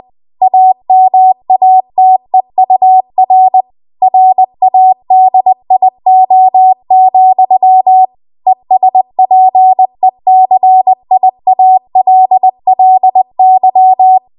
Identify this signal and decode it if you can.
Morse code, heard as a keyed tone, about 20 wpm, 750 Hz. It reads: AMATEUR RADIO, ESPECIALLY
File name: Archive / 20 WPM / 20WPM